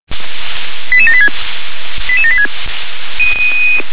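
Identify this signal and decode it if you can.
Five-tone selective call (ZVEI), heard as quick runs of five tones, tones of about 70 ms. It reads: 88786 88786
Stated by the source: ZVEI five-tone selective call received on an SDR.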